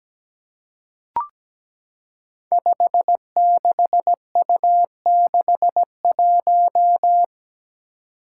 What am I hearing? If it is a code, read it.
Morse code, heard as a keyed tone, about 17 wpm, 700 Hz. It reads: 56U61
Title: Five Challenging Letter-Number Combinations 17wpm